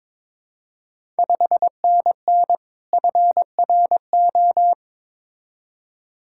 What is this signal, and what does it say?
Morse code, heard as a keyed tone, about 23 wpm, 690 Hz. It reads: 5NN FRO
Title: Ontario QSO Party Exchange 22wpm